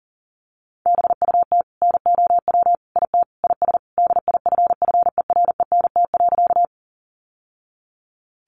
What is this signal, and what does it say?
Morse code, heard as a keyed tone, about 37 wpm, 700 Hz. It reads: BUT NOW IT IS DIFFERENT.
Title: Sentences from Top 600 Words 40wpm